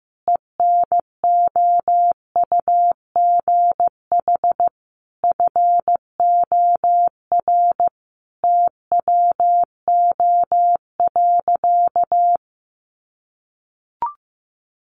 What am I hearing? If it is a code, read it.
Morse code, heard as a keyed tone, about 15 wpm, 700 Hz. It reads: ENOUGH FOR TWO.
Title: Sentences from Top 1400 Words 15wpm